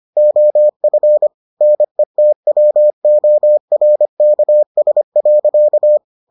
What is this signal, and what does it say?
Morse code, heard as a keyed tone, about 25 wpm, 600 Hz. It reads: OF NETWORKS.